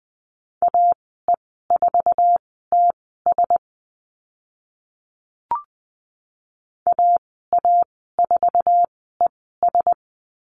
Morse code, heard as a keyed tone, about 20 wpm, spaced as for 15 wpm, 700 Hz. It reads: AE4TS AA4ES
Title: 069 Lesson 12g - Letter d - Call Signs 20wpm